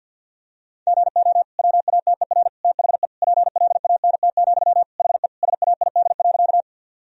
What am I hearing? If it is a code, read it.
Morse code, heard as a keyed tone, about 50 wpm, 700 Hz. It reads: GO WATER THE PLANT, HE SAID.
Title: Sentences from Top 1000 Words - Review Part 1 - 50wpm